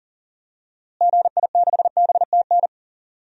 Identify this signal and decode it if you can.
Morse code, heard as a keyed tone, about 40 wpm, 700 Hz. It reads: GI6BTN
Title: DXCC-100 Call Signs - 6 Characters - Repeated 3x - 40wpm